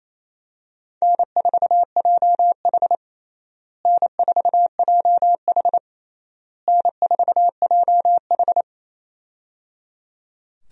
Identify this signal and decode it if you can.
Morse code, heard as a keyed tone, about 28 wpm, 700 Hz. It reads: N4JH N4JH N4JH
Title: USA Call Signs - 4 Characters - Repeated 3x - 28wpm